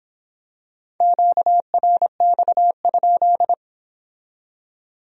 Morse code, heard as a keyed tone, about 26 wpm, 700 Hz. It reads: QRX?